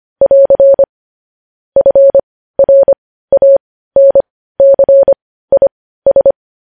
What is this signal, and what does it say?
Morse code, heard as a keyed tone, about 25 wpm, spaced as for 15 wpm, 550 Hz. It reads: <AR> FRANCIS
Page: News Headlines in Morse Code at 15 WPM